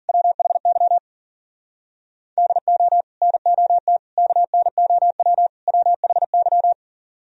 Morse code, heard as a keyed tone, about 49 wpm, 700 Hz. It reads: WHY DO NOT KNOW WHY
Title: Speed-Racing - Sets of 4 Words 40wpm